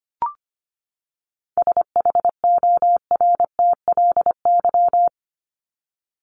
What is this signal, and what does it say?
Morse code, heard as a keyed tone, about 25 wpm, 700 Hz. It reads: SHORTLY
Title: LY Suffix - Top 100 Words with Base Words 25wpm